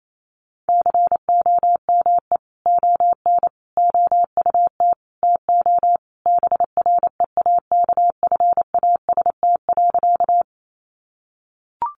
Morse code, heard as a keyed tone, about 28 wpm, 700 Hz. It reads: COME ON OUT TO BREAKFAST.
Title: Sentences from Top 1500 Words 28wpm